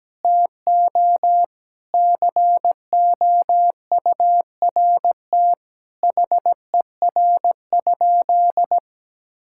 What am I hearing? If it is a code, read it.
Morse code, heard as a keyed tone, about 17 wpm, 700 Hz. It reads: TO COURT HER?
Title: Sentences from Top 1600 Words 17wpm